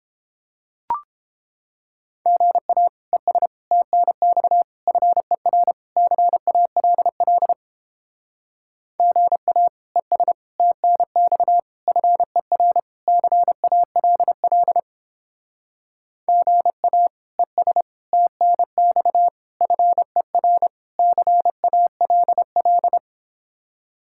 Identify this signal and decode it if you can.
Morse code, heard as a keyed tone, about 29 wpm, 700 Hz. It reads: GA ES TNX FER CALL GA ES TNX FER CALL GA ES TNX FER CALL